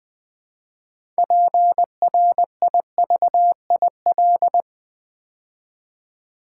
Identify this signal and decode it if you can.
Morse code, heard as a keyed tone, about 20 wpm, 700 Hz. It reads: PRIVIL
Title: Build A Word - 3 Letters At A Time - 9-Letter Words - 20wpm